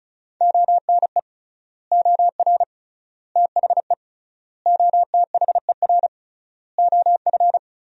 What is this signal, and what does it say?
Morse code, heard as a keyed tone, about 36 wpm, 700 Hz. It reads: ONE OR THE OTHER OF
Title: 3x Word-Spacing - Sentences from Top 400 Words - 35wpm